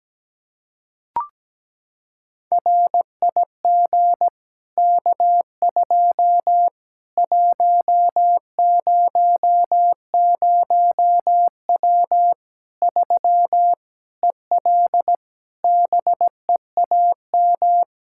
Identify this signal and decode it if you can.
Morse code, heard as a keyed tone, about 17 wpm, 700 Hz. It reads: RIG K2 100W 3 EL BEAM
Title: K1USN SST QSO Practice 17wpm